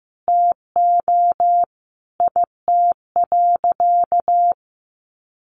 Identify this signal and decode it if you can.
Morse code, heard as a keyed tone, about 14 wpm, 700 Hz. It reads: TO IT.
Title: Sentences from Top 1200 Words 15wpm